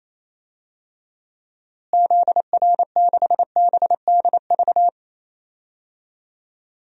Morse code, heard as a keyed tone, about 28 wpm, 700 Hz. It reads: ZR6BDV
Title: DXCC-100 Call Signs - 6 Characters - Repeated 2x - 28wpm